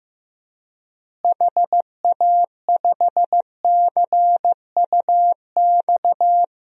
Morse code, heard as a keyed tone, about 15 wpm, 700 Hz. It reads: HA5CUX
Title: DXCC-100 Call Signs - 6 Characters - Repeated 2x - 15wpm